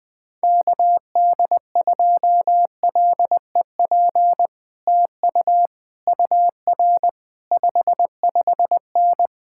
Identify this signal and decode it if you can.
Morse code, heard as a keyed tone, about 21 wpm, 700 Hz. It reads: KD2LEP TU UR 55N
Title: Speed-Racing Common QSO Phrases Encore - 20wpm